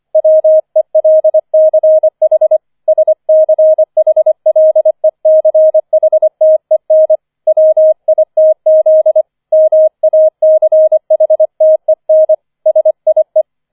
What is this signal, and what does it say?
Morse code, heard as a keyed tone, about 25 wpm, 630 Hz. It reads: WELCH SCHLECHTEN WITZ MACHTEN SIE
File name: 005 Fünfter Streich 25wpm